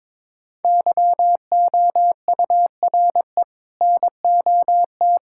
Morse code, heard as a keyed tone, about 22 wpm, 700 Hz. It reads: YOURE NOT
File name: practice / 22wpm / Sets of 7 Words - 22wpm